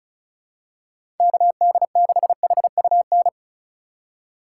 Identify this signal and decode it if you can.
Morse code, heard as a keyed tone, about 36 wpm, 700 Hz. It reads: KD6HUN
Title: USA Call Signs - 6 Characters - 4x Character Spacing - 35wpm